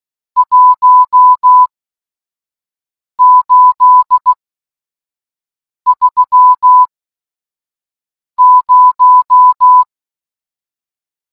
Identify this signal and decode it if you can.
Morse code, heard as a keyed tone, about 16 wpm, spaced as for 5 wpm, 1000 Hz. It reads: 1830